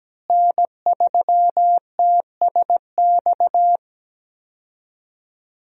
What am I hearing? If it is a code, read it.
Morse code, heard as a keyed tone, about 17 wpm, 700 Hz. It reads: N3TSX
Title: USA Call Signs - 5 Characters - Repeated 2x - 17wpm